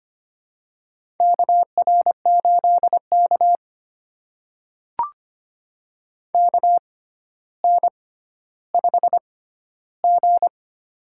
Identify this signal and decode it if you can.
Morse code, heard as a keyed tone, about 25 wpm, spaced as for 13 wpm, 700 Hz. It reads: KR8K KN5G
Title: USA Call Signs - 4 Characters - 6x Character Spacing - 25wpm